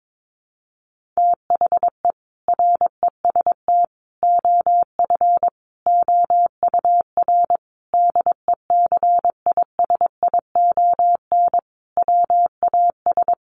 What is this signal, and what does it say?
Morse code, heard as a keyed tone, about 22 wpm, 700 Hz. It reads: THE REST OF OUR DECISION WAS
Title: Sentences from Top 1400 Words 22wpm